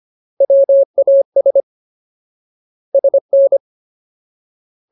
Morse code, heard as a keyed tone, about 25 wpm, 550 Hz. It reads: WAS SN
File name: Ultimate Word List / 25WPM Ultimate